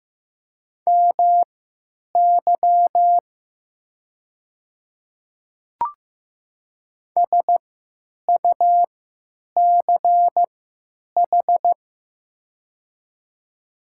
Morse code, heard as a keyed tone, about 15 wpm, spaced as for 9 wpm, 700 Hz. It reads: MY SUCH
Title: 3x Character Spacing - Top 200 Words - 15wpm